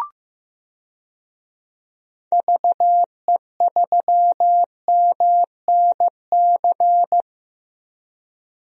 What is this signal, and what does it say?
Morse code, heard as a keyed tone, about 16 wpm, 700 Hz. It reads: VE3MNC
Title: DXCC-100 Call Signs - 6 Characters - Repeated 3x - 15wpm